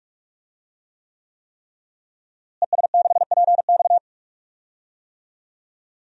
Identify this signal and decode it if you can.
Morse code, heard as a keyed tone, about 46 wpm, 700 Hz. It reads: ES6PX
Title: DXCC-100 Call Signs - 5 Characters - Repeated 3x - 45wpm